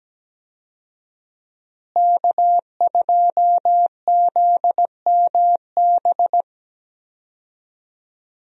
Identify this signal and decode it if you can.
Morse code, heard as a keyed tone, about 17 wpm, 700 Hz. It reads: K2ZMB
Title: USA Call Signs - 5 Characters - Repeated 3x - 17wpm